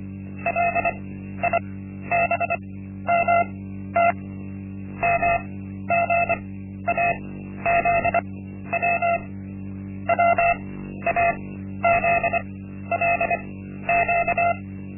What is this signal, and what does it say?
Morse code, heard as a keyed tone, about 25 wpm, spaced as for 12 wpm, 690 Hz. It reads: LIBMT MGAZW WAZLQ
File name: Linksmakalnis-1